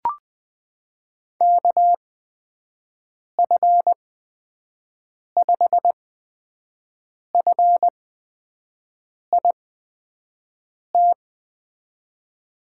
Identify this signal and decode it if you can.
Morse code, heard as a keyed tone, about 20 wpm, spaced as for 5 wpm, 700 Hz. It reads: KF5FIT
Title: USA Call Signs - 6 Characters - 8x Character Spacing - 20wpm